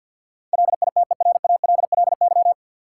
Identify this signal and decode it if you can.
Morse code, heard as a keyed tone, about 53 wpm, 700 Hz. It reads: LITERALLY
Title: Sentences from Top 1400 Words - New Word Drill Alt - 50wpm